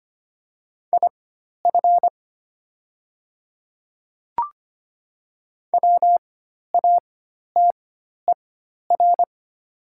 Morse code, heard as a keyed tone, about 25 wpm, spaced as for 12 wpm, 700 Hz. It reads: IF WATER
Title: 4x Character Spacing - Top 100 Words - 25wpm